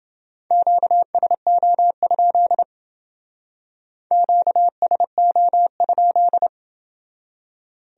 Morse code, heard as a keyed tone, about 28 wpm, 700 Hz. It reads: QSO? QSO?